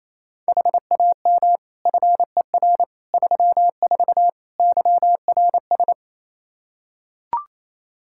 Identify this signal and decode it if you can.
Morse code, heard as a keyed tone, about 28 wpm, 700 Hz. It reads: HAM FER 34 YRS
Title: Practice QSO - Exchange 3 - More Info - 28wpm